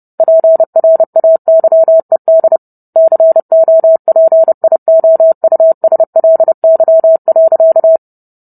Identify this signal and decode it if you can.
Morse code, heard as a keyed tone, about 30 wpm, 650 Hz. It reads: PRAYED COPIOUSLY.